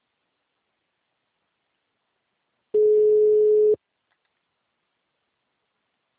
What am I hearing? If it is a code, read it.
Ringback tone (Europe, 425 Hz).
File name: out- -2003- - - .22